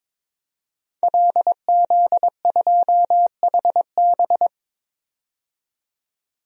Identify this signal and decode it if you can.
Morse code, heard as a keyed tone, about 22 wpm, 700 Hz. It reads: LZ2HB
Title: DXCC-100 Call Signs - 5 Characters - 22wpm